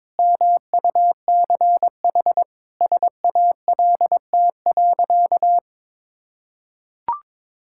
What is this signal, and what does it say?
Morse code, heard as a keyed tone, about 21 wpm, 700 Hz. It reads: MUCH SALT.
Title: Sentences from Top 1000 Words - Review Part 5 - 22wpm